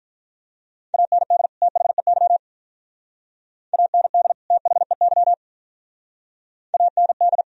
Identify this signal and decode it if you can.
Morse code, heard as a keyed tone, about 46 wpm, 690 Hz. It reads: AND THEY AND THEY AND